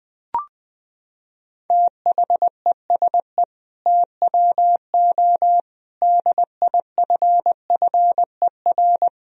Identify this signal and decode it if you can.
Morse code, heard as a keyed tone, about 21 wpm, 700 Hz. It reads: THESE TWO DIFFER
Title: Sentences from Top 1000 Words - Review Part 1 - 20wpm